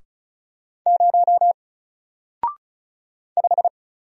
Morse code, heard as a keyed tone, about 36 wpm, 700 Hz. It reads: 0 5
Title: Single Character (Letter:Number:Punctuation:Prosign) Alt 35wpm